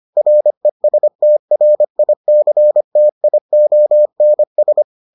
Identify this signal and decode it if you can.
Morse code, heard as a keyed tone, about 25 wpm, 600 Hz. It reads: RESTRICTIONS